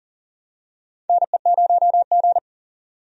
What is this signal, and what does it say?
Morse code, heard as a keyed tone, about 40 wpm, 700 Hz. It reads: NE0G
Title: USA Call Signs - 4 Characters - 40wpm